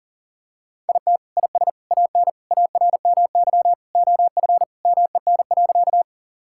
Morse code, heard as a keyed tone, about 40 wpm, 700 Hz. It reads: IT IS AN ARMY OF MEN.
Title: Sentences from Top 1000 Words - Review Part 5 - 40wpm